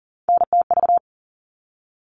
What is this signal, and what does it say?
Morse code, heard as a keyed tone, about 40 wpm, 700 Hz. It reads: NTV
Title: All Cut Numbers - 3-Digit 40wpm